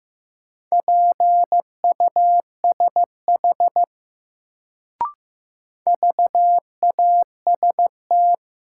Morse code, heard as a keyed tone, about 15 wpm, 700 Hz. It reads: PUSH VAST
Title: Sentences from Top 1300 Words - New Word Drill - 15wpm